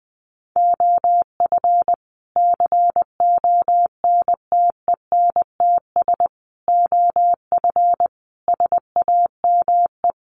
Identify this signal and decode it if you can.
Morse code, heard as a keyed tone, about 20 wpm, 700 Hz. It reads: OF CONTENTS OF SAME
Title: Sets of 5 Words Encore2 20wpm